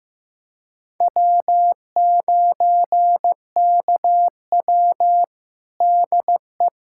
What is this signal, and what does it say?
Morse code, heard as a keyed tone, about 15 wpm, 700 Hz. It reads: W9KW DE
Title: USA Callsign DE Callsign 15wpm